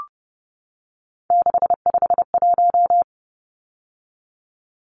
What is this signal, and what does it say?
Morse code, heard as a keyed tone, about 30 wpm, 700 Hz. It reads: 651
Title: Numbers - 3-Digit 30wpm